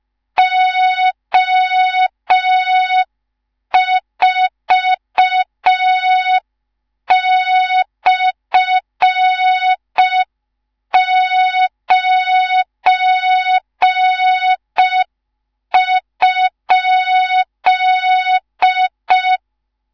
Morse code, heard as a keyed tone, about 5 wpm, 3700 Hz. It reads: O4/9?
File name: Morze-HA2MN / hang / 19 GYAKORLAT